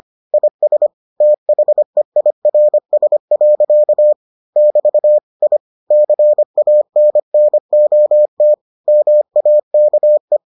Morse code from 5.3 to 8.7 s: I CANNOT

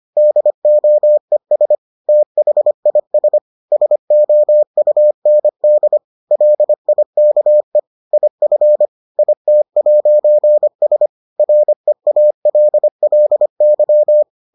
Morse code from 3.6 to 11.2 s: SOUND LIKE IF IT'S

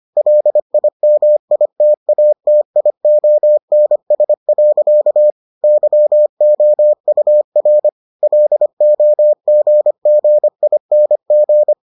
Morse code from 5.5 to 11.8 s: YOUR LOGGING

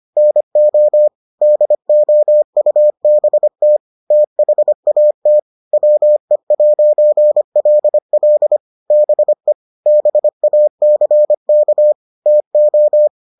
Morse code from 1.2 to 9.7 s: DOUBT THAT WE'LL BE

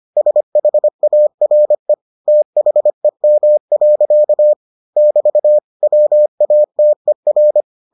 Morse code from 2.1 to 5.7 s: THEM. <BT>

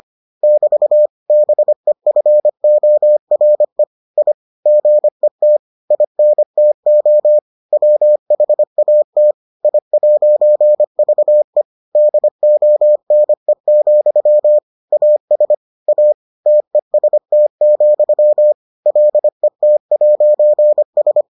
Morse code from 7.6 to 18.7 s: WHAT I'VE DONE, AS A TEST,

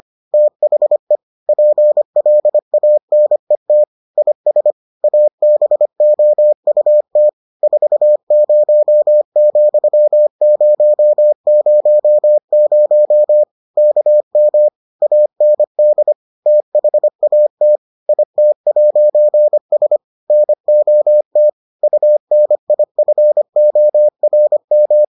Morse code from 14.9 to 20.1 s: AND THAT IT'S

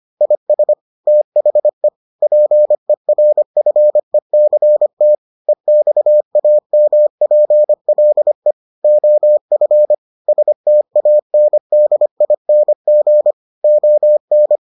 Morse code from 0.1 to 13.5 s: IS THE PERFECT EXAMPLE OF STANDING

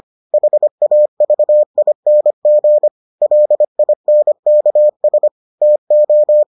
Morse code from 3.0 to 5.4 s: LINKS

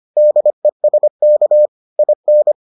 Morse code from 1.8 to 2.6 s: IN

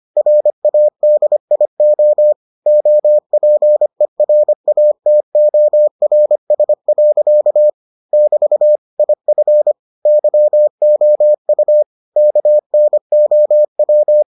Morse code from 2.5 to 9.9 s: OPERATORS. <BT> IF

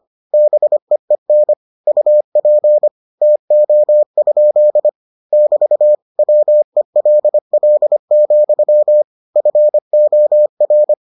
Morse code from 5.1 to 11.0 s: <BT> WELL, FOR

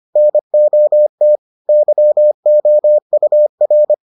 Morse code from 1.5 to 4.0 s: YOUR

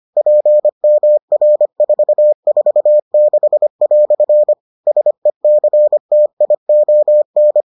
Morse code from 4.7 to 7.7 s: SECTION

